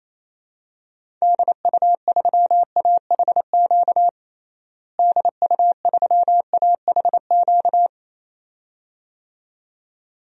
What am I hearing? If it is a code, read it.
Morse code, heard as a keyed tone, about 28 wpm, 700 Hz. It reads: DU3AHQ DU3AHQ
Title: DXCC-100 Call Signs - 6 Characters - Repeated 2x - 28wpm